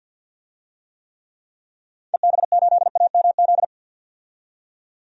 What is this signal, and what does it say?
Morse code, heard as a keyed tone, about 50 wpm, 700 Hz. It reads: EB8AMZ
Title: DXCC-100 Call Signs - 6 Characters - 50wpm